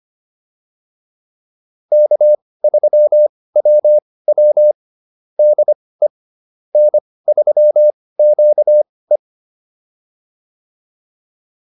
Morse code, heard as a keyed tone, about 25 wpm, spaced as for 18 wpm, 600 Hz. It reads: K3WW DE N3QE